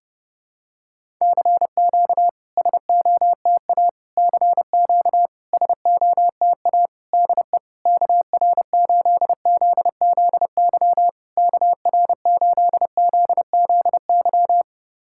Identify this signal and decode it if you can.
Morse code, heard as a keyed tone, about 30 wpm, 700 Hz. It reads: CQ SOTA CQ SOTA DE KR8ZZY KR8ZZY
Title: SOTA QSO Practice 30wpm